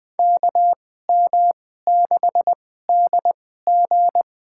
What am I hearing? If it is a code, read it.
Morse code, heard as a keyed tone, about 20 wpm, spaced as for 14 wpm, 700 Hz. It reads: KM6DG
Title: USA Call Signs - 5 Characters - 2x Character Spacing - 20wpm